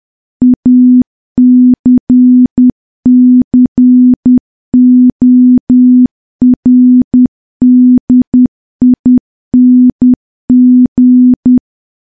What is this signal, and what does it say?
Morse code, heard as a keyed tone, about 10 wpm, 250 Hz. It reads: ACCORDING
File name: morse / morse-training-10WPM